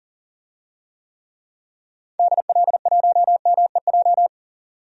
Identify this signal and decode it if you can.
Morse code, heard as a keyed tone, about 40 wpm, 700 Hz. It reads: DL1MEJ
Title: DXCC-100 Call Signs - 6 Characters - 40wpm